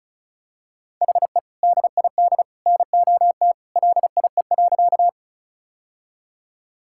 Morse code, heard as a keyed tone, about 36 wpm, 700 Hz. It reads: HE DID NOT LIE.
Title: Sentences from Top 1600 Words 35wpm